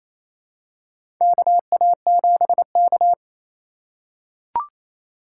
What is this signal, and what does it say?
Morse code, heard as a keyed tone, about 28 wpm, 700 Hz. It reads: KA7K
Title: USA Call Signs - 4 Characters - 3x Character Spacing - 28wpm